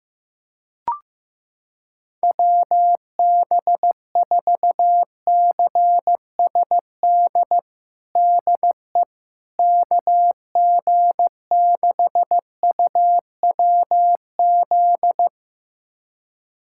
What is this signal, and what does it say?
Morse code, heard as a keyed tone, about 15 wpm, 700 Hz. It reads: WB4CSD DE KG6UWZ
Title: USA Callsign DE Callsign 15wpm